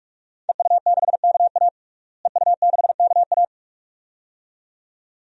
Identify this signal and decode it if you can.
Morse code, heard as a keyed tone, about 42 wpm, 700 Hz. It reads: EU6KA EU6KA
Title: DXCC-100 Call Signs - 5 Characters - Repeated 2x - 45wpm